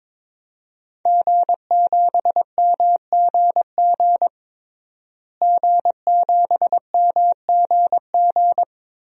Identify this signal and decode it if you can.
Morse code, heard as a keyed tone, about 22 wpm, 700 Hz. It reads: G7MGG G7MGG
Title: DXCC-100 Call Signs - 5 Characters - Repeated 2x - 22wpm